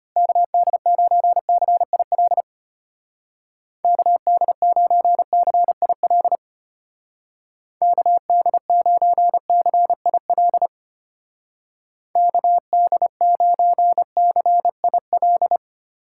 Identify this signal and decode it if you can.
Morse code, heard as a keyed tone, about 33 wpm, 700 Hz. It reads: KD9CIL KD9CIL KD9CIL KD9CIL